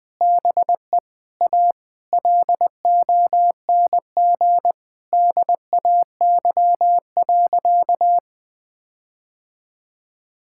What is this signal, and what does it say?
Morse code, heard as a keyed tone, about 21 wpm, 700 Hz. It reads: BE A LONG DAY.
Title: Sentences with 7-Word Phrases - 20wpm